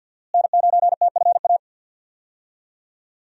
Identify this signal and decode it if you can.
Morse code, heard as a keyed tone, about 51 wpm, 700 Hz. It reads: N9TFA
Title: USA Call Signs - 5 Characters - Repeated 3x - 50wpm